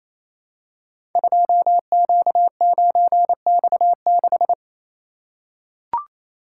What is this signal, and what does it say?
Morse code, heard as a keyed tone, about 28 wpm, 700 Hz. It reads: 2Q9X6
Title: Code Groups - 5 Letter:Numbers - 28wpm